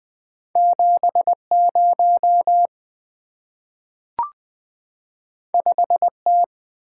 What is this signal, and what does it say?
Morse code, heard as a keyed tone, about 20 wpm, 700 Hz. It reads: 70 5T
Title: Two Letter-Number Combinations - 20wpm